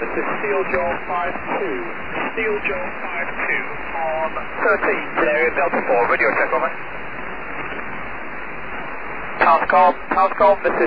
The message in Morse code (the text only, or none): none